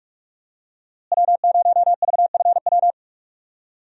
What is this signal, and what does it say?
Morse code, heard as a keyed tone, about 47 wpm, 700 Hz. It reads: W0VFW
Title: USA Call Signs - 5 Characters - Repeated 2x - 45wpm